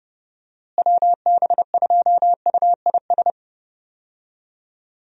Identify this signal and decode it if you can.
Morse code, heard as a keyed tone, about 29 wpm, 700 Hz. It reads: WB2UIS
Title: USA Call Signs - 6 Characters - Repeated 2x - 30wpm